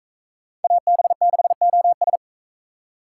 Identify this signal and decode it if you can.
Morse code, heard as a keyed tone, about 42 wpm, 700 Hz. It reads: AB6OS